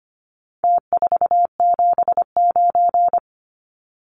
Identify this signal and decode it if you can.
Morse code, heard as a keyed tone, about 25 wpm, 700 Hz. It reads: T479
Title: Numbers - Contest Style - 4-Digit 25wpm